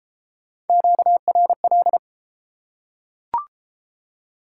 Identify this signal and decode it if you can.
Morse code, heard as a keyed tone, about 28 wpm, 700 Hz. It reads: QRL